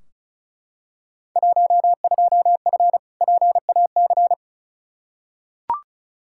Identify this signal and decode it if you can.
Morse code, heard as a keyed tone, about 35 wpm, 700 Hz. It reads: 12F PAC
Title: Field Day Exchanges Encore - 35wpm